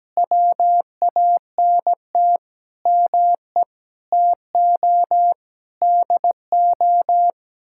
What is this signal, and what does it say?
Morse code, heard as a keyed tone, about 17 wpm, 700 Hz. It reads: WANT ME TO DO
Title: Sentences from Top 1000 Words - Review Part 6 - 17wpm